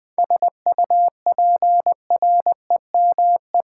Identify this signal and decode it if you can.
Morse code, heard as a keyed tone, about 20 wpm, 700 Hz. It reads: SUPREME
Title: 7 Letter Words - First 3 Letters Given - 20wpm